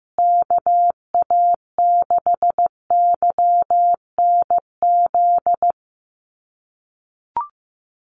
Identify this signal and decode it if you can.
Morse code, heard as a keyed tone, about 15 wpm, 700 Hz. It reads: KA6YNZ
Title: USA Call Signs Repeated 2x 15wpm